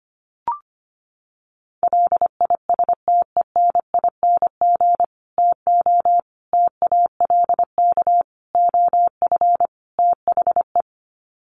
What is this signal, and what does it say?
Morse code, heard as a keyed tone, about 25 wpm, 700 Hz. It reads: LISTENING TO TALK OF THE
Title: Sets of 5 Words 25wpm